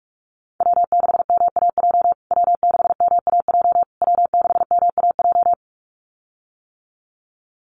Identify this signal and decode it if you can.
Morse code, heard as a keyed tone, about 44 wpm, 700 Hz. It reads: W6MAJ W6MAJ W6MAJ
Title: USA Call Signs Repeated 3x 45wpm